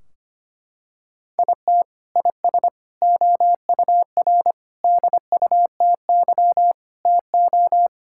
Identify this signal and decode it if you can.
Morse code, heard as a keyed tone, about 25 wpm, 700 Hz. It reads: IT IS OUR DUTY TO
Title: Sentences from Top 1600 Words 25wpm